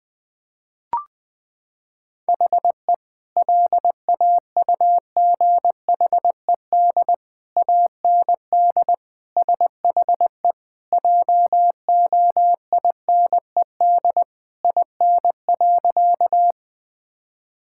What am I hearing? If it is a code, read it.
Morse code, heard as a keyed tone, about 21 wpm, 700 Hz. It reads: HE LAUGHED AND SHE JOINED IN.
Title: Sentences with 2-Word Phrases - 20wpm